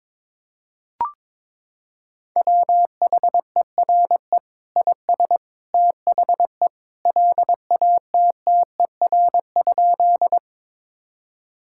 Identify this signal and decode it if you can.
Morse code, heard as a keyed tone, about 21 wpm, 700 Hz. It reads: WHERE IS THE LATTER?
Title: Sentences from Top 1600 Words 22wpm